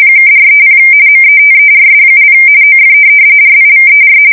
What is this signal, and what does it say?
RTTY, about 45.45 baud. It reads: (430 METERS) OF RIVER
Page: 170 Shift 2125/2295